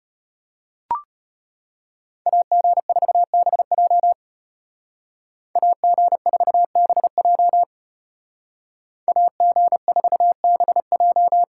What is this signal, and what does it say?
Morse code, heard as a keyed tone, about 33 wpm, 700 Hz. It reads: AG4BJ AG4BJ AG4BJ